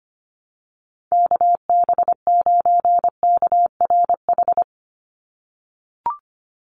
Morse code, heard as a keyed tone, about 24 wpm, 700 Hz. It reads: KB9KRH
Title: USA Call Signs 25wpm